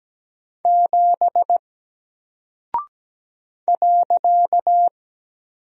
Morse code, heard as a keyed tone, about 17 wpm, 700 Hz. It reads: 7 .
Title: Single Character (Letter:Number:Punctuation:Prosign) Alt 17wpm